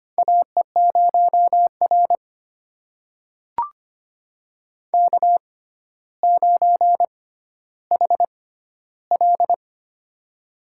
Morse code, heard as a keyed tone, about 25 wpm, spaced as for 13 wpm, 700 Hz. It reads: AE0R K9HL
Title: USA Call Signs - 4 Characters - 6x Character Spacing - 25wpm